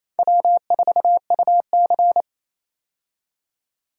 Morse code, heard as a keyed tone, about 28 wpm, 700 Hz. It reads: W4UC
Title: USA Call Signs - 4 Characters - 28wpm